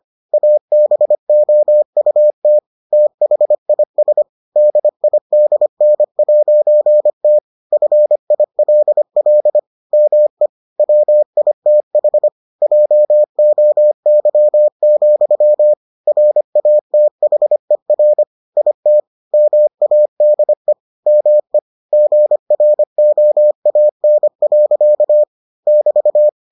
Morse code, heard as a keyed tone, about 25 wpm, 600 Hz. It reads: ABOUT THIS DIDN'T FILL ME WITH JOY, RATHER IT MADE ME GROAN. <BT>